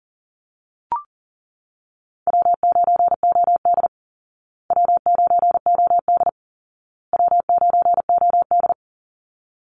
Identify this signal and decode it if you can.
Morse code, heard as a keyed tone, about 40 wpm, 700 Hz. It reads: W9OD W9OD W9OD
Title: Call Signs Repeated 2x - 40wpm